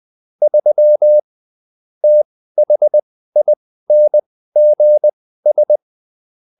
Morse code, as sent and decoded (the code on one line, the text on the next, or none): ...-- / - .... .. -. --. ...
3 THINGS